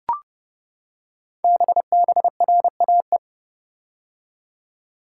Morse code, heard as a keyed tone, about 29 wpm, 700 Hz. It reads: BBRAE
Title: Code Groups - 5 Letters - 30wpm